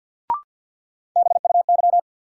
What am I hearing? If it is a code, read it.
Morse code, heard as a keyed tone, about 48 wpm, 700 Hz. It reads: BUY
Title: Sentences from Top 700 Words - New Word Drill - 50wpm